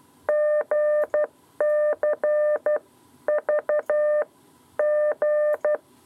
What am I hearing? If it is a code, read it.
Morse code, heard as a keyed tone, about 12 wpm, 600 Hz. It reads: GC V G
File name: 7b lezione cw 2